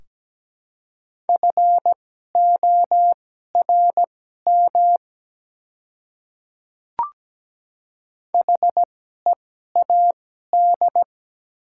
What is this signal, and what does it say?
Morse code, heard as a keyed tone, about 17 wpm, spaced as for 12 wpm, 700 Hz. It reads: FORM HEAD
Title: 2x Character Spacing - Top 300 Words - 17wpm